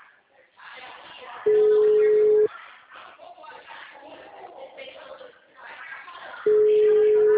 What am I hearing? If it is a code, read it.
Ringback tone (Europe, 425 Hz), with speech.
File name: out- -2005- - - .66138